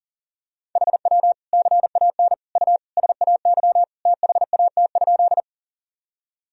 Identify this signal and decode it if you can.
Morse code, heard as a keyed tone, about 40 wpm, 700 Hz. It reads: HW CAN U SAY THAT?
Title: Sentences from Top 200 Words and Abbreviations 40wpm